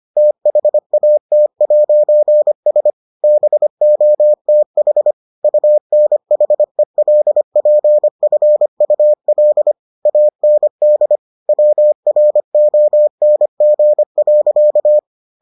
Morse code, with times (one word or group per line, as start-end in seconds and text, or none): 0.2-2.9 THAT'S
3.2-5.1 BOTH
5.4-9.7 UNHELPFUL
10.1-11.2 AND
11.5-15.0 WRONG.